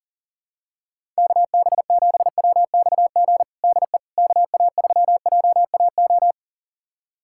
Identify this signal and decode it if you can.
Morse code, heard as a keyed tone, about 40 wpm, 700 Hz. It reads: KB7WXG DE KA3JAO
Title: USA Callsign DE Callsign 40wpm